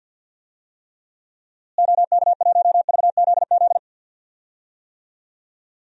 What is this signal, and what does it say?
Morse code, heard as a keyed tone, about 47 wpm, 700 Hz. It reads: OK1VZZ
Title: DXCC-100 Call Signs - 6 Characters - Repeated 2x - 50wpm